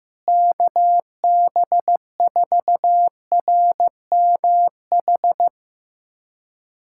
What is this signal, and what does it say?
Morse code, heard as a keyed tone, about 15 wpm, 700 Hz. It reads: KB4RMH
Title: USA Callsign DE Callsign 15wpm